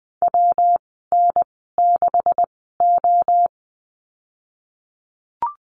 Morse code, heard as a keyed tone, about 20 wpm, spaced as for 14 wpm, 700 Hz. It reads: WN6O
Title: 128 Lesson 21g - Number 6 - Call Signs 20wpm